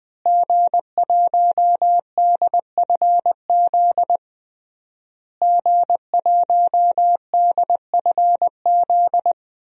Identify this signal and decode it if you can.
Morse code, heard as a keyed tone, about 20 wpm, 700 Hz. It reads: G1DFZ G1DFZ
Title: DXCC-100 Call Signs - 5 Characters - Repeated 3x - 20wpm